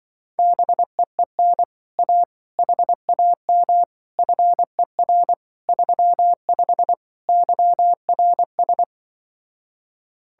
Morse code, heard as a keyed tone, about 25 wpm, 700 Hz. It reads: BEEN A HAM FER 35 YRS
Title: Speed-Racing Common QSO Phrases Encore - 20wpm